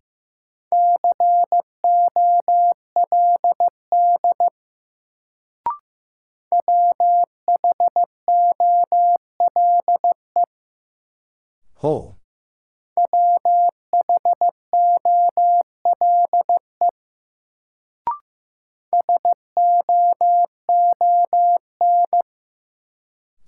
Morse code, heard as a keyed tone, about 15 wpm, 700 Hz. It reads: COLD WHOLE WHOLE SOON
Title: Sentences from Top 400 Words - New Word Drill - 15wpm